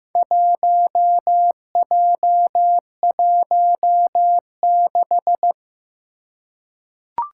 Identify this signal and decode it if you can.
Morse code, heard as a keyed tone, about 15 wpm, 700 Hz. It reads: 1J16
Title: Four Challenging Letter-Number Combinations 15wpm